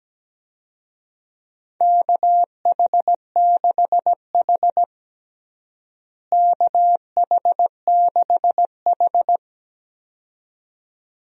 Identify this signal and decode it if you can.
Morse code, heard as a keyed tone, about 17 wpm, 700 Hz. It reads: KH6H KH6H
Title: DXCC-100 Call Signs - 4 Characters - Repeated 2x - 17wpm